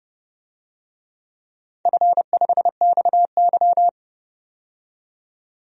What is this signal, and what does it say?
Morse code, heard as a keyed tone, about 28 wpm, 700 Hz. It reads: F5XY
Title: DXCC-100 Call Signs - 4 Characters - 30wpm